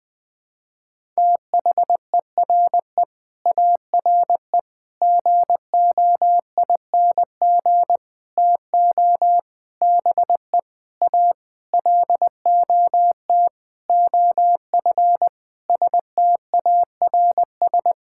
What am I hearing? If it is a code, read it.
Morse code, heard as a keyed tone, about 20 wpm, 700 Hz. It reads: THERE ARE GOING TO BE A LOT OF STARS
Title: Sentences with 8-Word Phrases - 20wpm